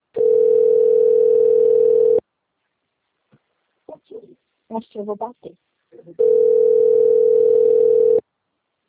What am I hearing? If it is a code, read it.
Ringback tone (North America), 440 + 480 Hz, with speech.
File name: rg-1600-unknown- - - .21735